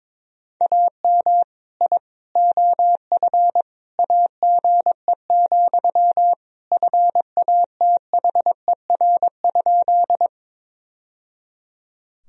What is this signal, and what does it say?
Morse code, heard as a keyed tone, about 22 wpm, 700 Hz. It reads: AM I OF AGE, FATHER?
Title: Sentences from Top 1000 Words - Review Part 3 - 22wpm